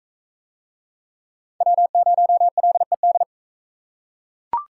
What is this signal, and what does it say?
Morse code, heard as a keyed tone, about 44 wpm, 700 Hz. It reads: W0LED